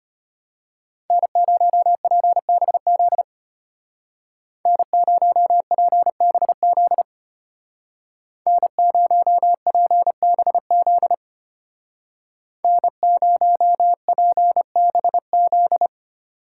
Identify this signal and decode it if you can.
Morse code, heard as a keyed tone, about 33 wpm, 700 Hz. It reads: N0PBZ N0PBZ N0PBZ N0PBZ